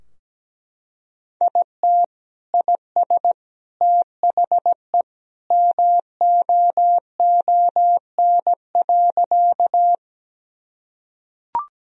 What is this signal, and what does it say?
Morse code, heard as a keyed tone, about 17 wpm, 700 Hz. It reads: IT IS THE MOON.
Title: Sentences from Top 1000 Words - Review Part 3 - 17wpm